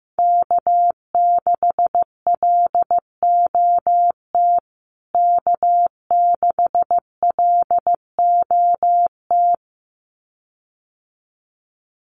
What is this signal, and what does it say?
Morse code, heard as a keyed tone, about 15 wpm, 700 Hz. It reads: K6LOT K6LOT
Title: USA Call Signs Repeated 2x 15wpm